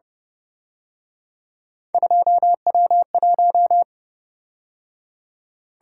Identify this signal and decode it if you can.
Morse code, heard as a keyed tone, about 30 wpm, 700 Hz. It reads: 2W1
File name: Speed-Racing - Three Letter-Number Combinations - 25wpm